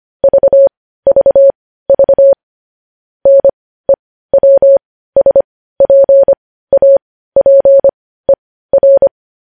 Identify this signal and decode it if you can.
Morse code, heard as a keyed tone, about 25 wpm, spaced as for 15 wpm, 550 Hz. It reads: VVV NEWSPAPER
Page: News Headlines in Morse Code at 15 WPM